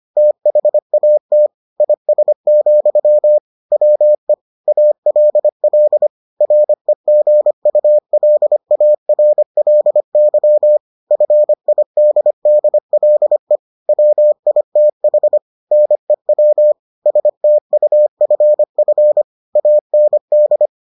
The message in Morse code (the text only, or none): THAT IS, WE ALL REGULARLY FIDDLE WITH NEW STUFF AND